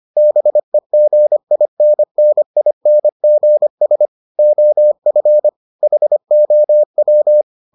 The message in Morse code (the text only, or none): BEGINNINGS OF HOW